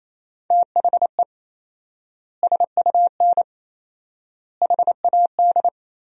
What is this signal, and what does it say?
Morse code, heard as a keyed tone, about 27 wpm, 700 Hz. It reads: THE SUN HAD
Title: 4x Word-Spacing - Sentences from Top 300 Words - 28wpm